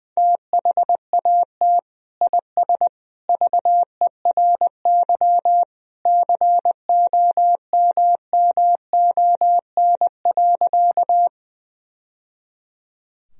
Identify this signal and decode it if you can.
Morse code, heard as a keyed tone, about 20 wpm, 700 Hz. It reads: THAT IS VERY COMMON.
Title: Sentences from Top 1000 Words - Review Part 3 - 20wpm